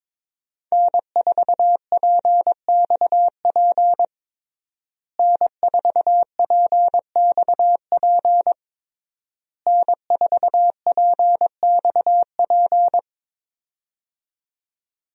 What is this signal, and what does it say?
Morse code, heard as a keyed tone, about 22 wpm, 700 Hz. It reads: N4PXP N4PXP N4PXP
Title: USA Call Signs - 5 Characters - Repeated 3x - 22wpm